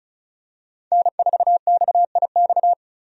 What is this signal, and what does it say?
Morse code, heard as a keyed tone, about 34 wpm, 700 Hz. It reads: N4XIX
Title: USA Call Signs - 5 Characters - 35wpm